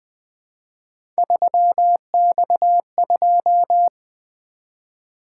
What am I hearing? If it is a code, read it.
Morse code, heard as a keyed tone, about 20 wpm, 700 Hz. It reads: 3X2
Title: Three Letter-Number Combinations - 20wpm